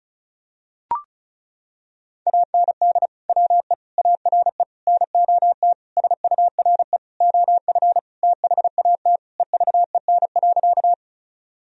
Morse code, heard as a keyed tone, about 34 wpm, 700 Hz. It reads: AND WE ARE NOT SURE OF THAT EVEN.
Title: Sentences from Top 1000 Words - Review Part 1 - 35wpm